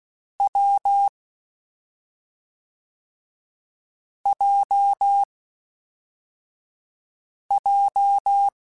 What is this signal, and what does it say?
Morse code, heard as a keyed tone, about 16 wpm, 800 Hz. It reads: W J J
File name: Difficult Combos / Difficult Combos-mp3s / awj1 ndb6 gz7 @5